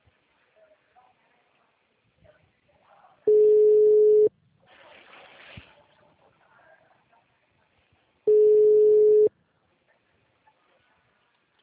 Ringback tone (Europe, 425 Hz).
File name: out- -2013- - - .86419